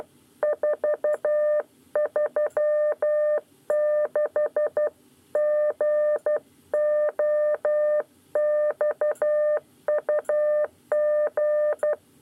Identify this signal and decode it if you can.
Morse code, heard as a keyed tone, about 12 wpm, 600 Hz. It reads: 436 GOXUG
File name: 9 lezione cw 2